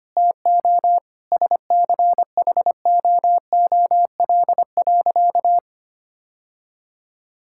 Morse code, heard as a keyed tone, about 25 wpm, 700 Hz. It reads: TO SCHOOL.
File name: practice / 25wpm / Sentences with 4-Word Phrases - 25wpm